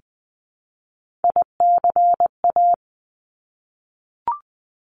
Morse code, heard as a keyed tone, about 20 wpm, 700 Hz. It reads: ICA
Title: Top Three Letter Combinations in Words 20wpm